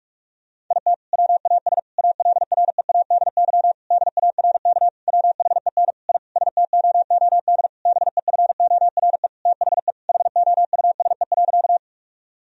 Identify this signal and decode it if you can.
Morse code, heard as a keyed tone, about 46 wpm, 700 Hz. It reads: IT WAS ALREADY DARK WHEN I STOOD BEFORE THE HOUSE.
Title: Sentences from Top 1000 Words - Review Part 2 - 45wpm